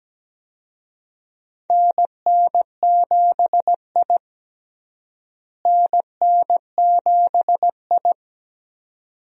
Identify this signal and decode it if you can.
Morse code, heard as a keyed tone, about 17 wpm, 700 Hz. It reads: NN7I NN7I
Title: USA Call Signs - 4 Characters - Repeated 2x - 17wpm